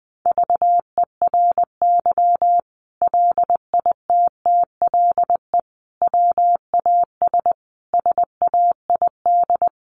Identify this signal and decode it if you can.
Morse code, heard as a keyed tone, about 20 wpm, 700 Hz. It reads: VERY LITTLE WAS SAID
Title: Sentences from Top 600 Words 20wpm